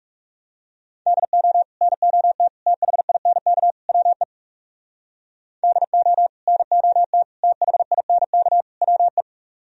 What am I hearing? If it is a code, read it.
Morse code, heard as a keyed tone, about 38 wpm, 700 Hz. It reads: DO NOT THINK WE DO NOT THINK WE